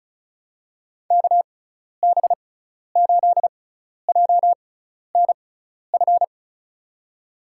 Morse code, heard as a keyed tone, about 36 wpm, spaced as for 12 wpm, 700 Hz. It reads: KB8JNF
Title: USA Call Signs - 6 Characters - 6x Character Spacing - 35wpm